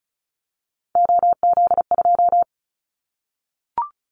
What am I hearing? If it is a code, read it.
Morse code, heard as a keyed tone, about 34 wpm, 700 Hz. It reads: OZ2
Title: Callsign Prefixes for Top 50 DXCC - 35wpm